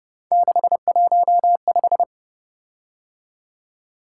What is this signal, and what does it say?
Morse code, heard as a keyed tone, about 29 wpm, 700 Hz. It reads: 615
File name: Speed-Racing - Three Letter-Number Combinations - 25wpm